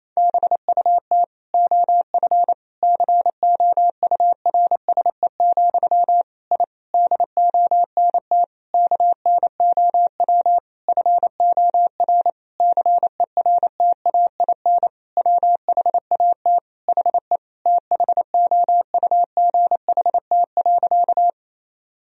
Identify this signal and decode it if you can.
Morse code, heard as a keyed tone, about 27 wpm, 700 Hz. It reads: BUT OF COURSE, I DONT KNOW FOR CERTAIN WHAT HE THOUGHT.
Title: Sentences from Top 1000 Words - Review Part 2 - 28wpm